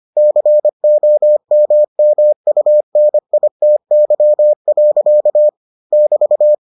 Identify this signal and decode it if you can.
Morse code, heard as a keyed tone, about 25 wpm, 600 Hz. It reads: COMMUNITY. <BT>